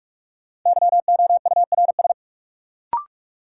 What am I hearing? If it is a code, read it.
Morse code, heard as a keyed tone, about 43 wpm, 700 Hz. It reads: YOURS
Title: Sentences from Top 1500 Words - New Word Drill Alt - 45wpm